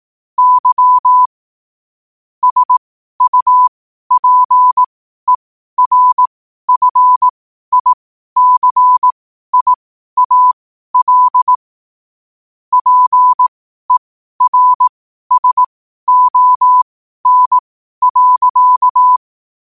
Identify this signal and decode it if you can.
Morse code, heard as a keyed tone, about 18 wpm, spaced as for 12 wpm, 1000 Hz. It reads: Y SUPERFICIAL PERSON.
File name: 12wpm